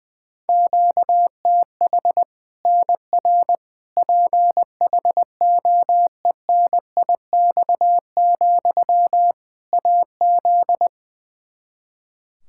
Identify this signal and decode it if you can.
Morse code, heard as a keyed tone, about 20 wpm, 700 Hz. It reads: QTH NR PHOENIX, AZ
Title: Practice QSO - Exchange 1 - Beginning - 20wpm